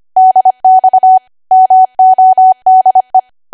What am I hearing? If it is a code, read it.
Morse code, heard as a keyed tone, about 25 wpm, 750 Hz. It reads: DX MODE